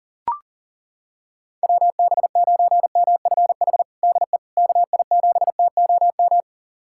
Morse code, heard as a keyed tone, about 40 wpm, 700 Hz. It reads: WB9MFH DE KI7TOM
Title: USA Callsign DE Callsign - 6 Characters - 40wpm